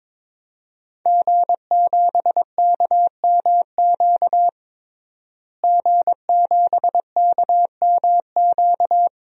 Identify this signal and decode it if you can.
Morse code, heard as a keyed tone, about 22 wpm, 700 Hz. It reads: G7KMQ G7KMQ
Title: DXCC-100 Call Signs - 5 Characters - Repeated 2x - 22wpm